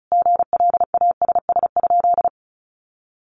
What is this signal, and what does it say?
Morse code, heard as a keyed tone, about 34 wpm, 700 Hz. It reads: GLASS?
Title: Sentences from Top 1300 Words 35wpm